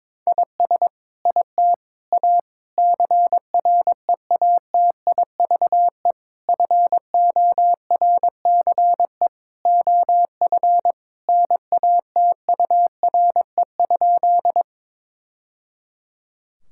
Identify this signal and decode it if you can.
Morse code, heard as a keyed tone, about 22 wpm, 700 Hz. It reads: IS IT A CREATIVE FORCE OF NATURE?
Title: Sentences from Top 1000 Words - Review Part 5 - 22wpm